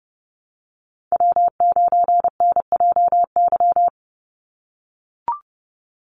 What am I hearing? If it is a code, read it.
Morse code, heard as a keyed tone, about 28 wpm, 700 Hz. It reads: W9NJY
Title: Call Signs - 30wpm